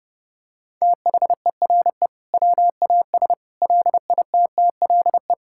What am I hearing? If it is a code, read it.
Morse code, heard as a keyed tone, about 30 wpm, 700 Hz. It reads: THERE WAS LITTLE
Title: Sentences from Top 1000 Words - Review Part 6 - 30wpm